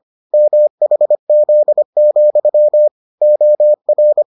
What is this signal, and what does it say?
Morse code, heard as a keyed tone, about 25 wpm, 600 Hz. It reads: MHZ, OR